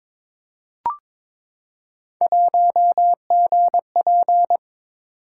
Morse code, heard as a keyed tone, about 22 wpm, 700 Hz. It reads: 1GP
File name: practice / 22wpm / Three Letter-Number Combinations - 22wpm